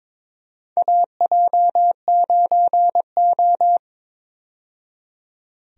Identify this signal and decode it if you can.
Morse code, heard as a keyed tone, about 22 wpm, 700 Hz. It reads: AJ9O
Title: USA Call Signs - 4 Characters - 22wpm